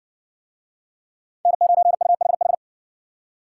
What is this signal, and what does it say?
Morse code, heard as a keyed tone, about 60 wpm, 700 Hz. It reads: N9UHH